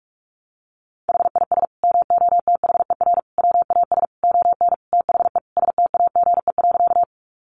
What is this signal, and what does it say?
Morse code, heard as a keyed tone, about 44 wpm, 700 Hz. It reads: HIS MOTHER WAS ON THE STAGE.
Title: Sentences from Top 1200 Words 45wpm